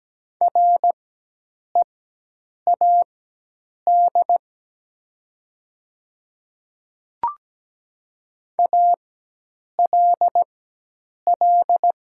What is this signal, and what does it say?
Morse code, heard as a keyed tone, about 17 wpm, spaced as for 8 wpm, 700 Hz. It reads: READ ALL
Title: 4x Character Spacing - Top 300 Words - 17wpm